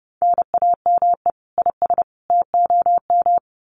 Morse code, heard as a keyed tone, about 30 wpm, 700 Hz. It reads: NAME IS TOM
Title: Common QSO Phrases 30wpm